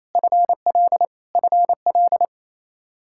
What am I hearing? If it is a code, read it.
Morse code, heard as a keyed tone, about 28 wpm, 700 Hz. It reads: FL FL
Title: Practice QSO - Exchange 1 - Beginning - 28wpm